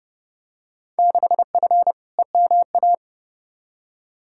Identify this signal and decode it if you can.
Morse code, heard as a keyed tone, about 31 wpm, 700 Hz. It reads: 6F EMA
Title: Field Day Exchanges Encore - 30wpm